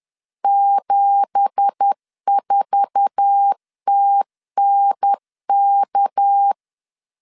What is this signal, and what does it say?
Morse code, heard as a keyed tone, about 10 wpm, 800 Hz. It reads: 74TNK
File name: LEZIONI / 9 2867 10wpm 2733